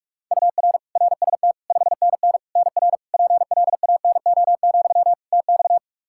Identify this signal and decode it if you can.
Morse code, heard as a keyed tone, about 42 wpm, 700 Hz. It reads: UR RST 5NN NR PLANO, TX
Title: Practice QSO - Exchange 1 - Beginning - 45wpm